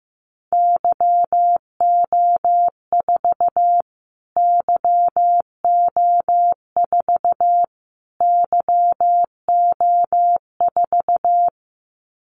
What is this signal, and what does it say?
Morse code, heard as a keyed tone, about 15 wpm, 700 Hz. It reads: YO4 YO4 YO4
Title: Callsign Prefixes for Top 25 DXCC - Repeated 3x - 15wpm